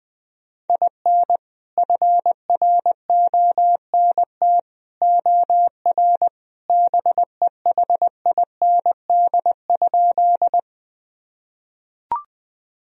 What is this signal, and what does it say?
Morse code, heard as a keyed tone, about 20 wpm, 700 Hz. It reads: IN FRONT OR BEHIND?
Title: Sentences from Top 1000 Words - Review Part 2 - 20wpm